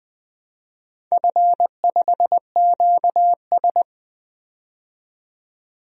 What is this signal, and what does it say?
Morse code, heard as a keyed tone, about 20 wpm, 700 Hz. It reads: F5QS
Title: DXCC-100 Call Signs - 4 Characters - 20wpm